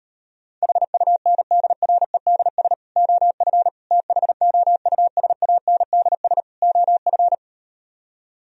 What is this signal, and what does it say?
Morse code, heard as a keyed tone, about 38 wpm, 700 Hz. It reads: HUNDREDS OF THOUSANDS OF